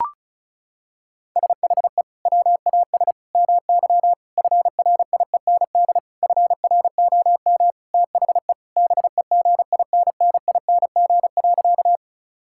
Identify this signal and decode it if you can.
Morse code, heard as a keyed tone, about 34 wpm, 700 Hz. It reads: SHE WAS MY FRIEND FROM THE BEGINNING.
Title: Sentences from Top 1000 Words - Review Part 5 - 35wpm